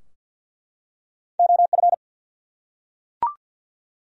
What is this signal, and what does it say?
Morse code, heard as a keyed tone, about 50 wpm, 700 Hz. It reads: OF
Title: Top 100 Words 50wpm